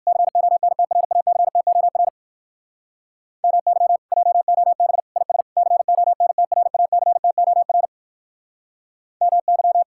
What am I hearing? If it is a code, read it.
Morse code, heard as a keyed tone, about 58 wpm, 700 Hz. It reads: CONTRACTOR MY JOB IS CONTRACTOR MY